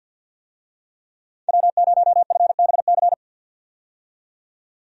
Morse code, heard as a keyed tone, about 48 wpm, 700 Hz. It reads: W0FBC
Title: USA Call Signs - 5 Characters - 50wpm